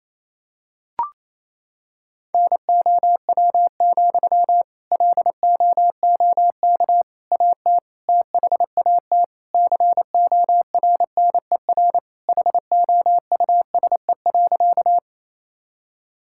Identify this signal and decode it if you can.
Morse code, heard as a keyed tone, about 28 wpm, 700 Hz. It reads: NOW, LOOK AT THAT CORNER HOUSE.
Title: Sentences from Top 1600 Words 28wpm